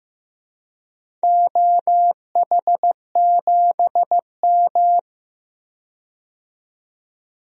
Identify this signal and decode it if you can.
Morse code, heard as a keyed tone, about 15 wpm, 700 Hz. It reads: OH7M
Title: DXCC-100 Call Signs - 4 Characters - 15wpm